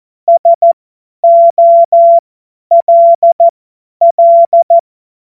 Morse code, heard as a keyed tone, about 14 wpm, spaced as for 10 wpm, 680 Hz. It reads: SOLL
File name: Morsekurs / Witze / Witz 08. August 2023 50 F70